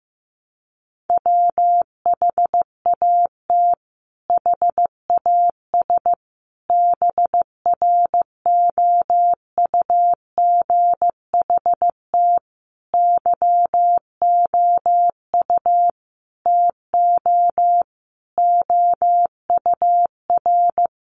Morse code, heard as a keyed tone, about 15 wpm, 700 Hz. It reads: WHAT HAS BROUGHT YOU TO OUR
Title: Sentences from Top 1500 Words 15wpm